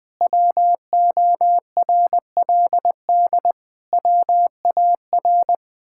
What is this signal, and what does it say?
Morse code, heard as a keyed tone, about 21 wpm, 700 Hz. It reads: WORLD WAR
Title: Sentences with 7-Word Phrases - 20wpm